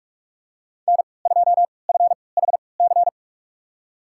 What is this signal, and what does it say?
Morse code, heard as a keyed tone, about 46 wpm, spaced as for 26 wpm, 700 Hz. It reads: N2FHC
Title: USA Call Signs - 5 Characters - 3x Character Spacing - 45wpm